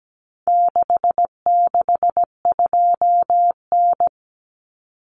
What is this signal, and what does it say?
Morse code, heard as a keyed tone, about 17 wpm, 700 Hz. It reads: 662N
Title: Common Cut Numbers - 4-Digit 17wpm